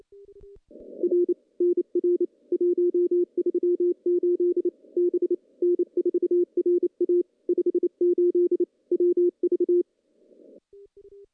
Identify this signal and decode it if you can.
Morse code, heard as a keyed tone, about 29 wpm, 350 Hz. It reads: R NR 138 B N4RA 58 WV
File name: contests / CW-SS-k5tr- -0100